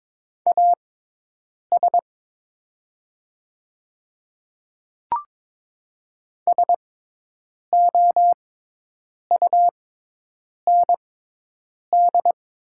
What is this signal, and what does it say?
Morse code, heard as a keyed tone, about 22 wpm, spaced as for 8 wpm, 700 Hz. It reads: AS SOUND
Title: 6x Character Spacing - Top 100 Words - 22wpm